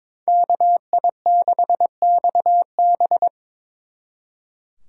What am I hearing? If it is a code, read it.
Morse code, heard as a keyed tone, about 26 wpm, 700 Hz. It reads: KI6XB
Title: Speed Racing - USA 2x2 Call Signs 22wpm